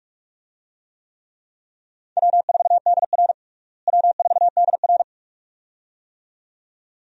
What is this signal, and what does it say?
Morse code, heard as a keyed tone, about 48 wpm, 700 Hz. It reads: W4DR W4DR
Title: USA Call Signs - 4 Characters - Repeated 2x - 45wpm